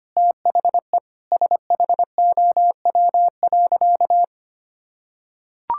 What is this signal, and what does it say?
Morse code, heard as a keyed tone, about 25 wpm, 700 Hz. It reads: THE SHOW.
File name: practice / 25wpm / Sentences with 4-Word Phrases - 25wpm